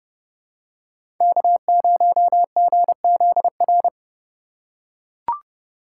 Morse code, heard as a keyed tone, about 30 wpm, 700 Hz. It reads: K0GZR
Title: USA Call Signs - 5 Characters - 4x Character Spacing - 30wpm